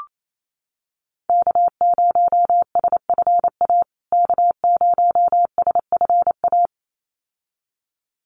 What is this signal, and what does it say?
Morse code, heard as a keyed tone, about 28 wpm, 700 Hz. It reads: K0SFA K0SFA
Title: USA Call Signs Repeated 2x 28wpm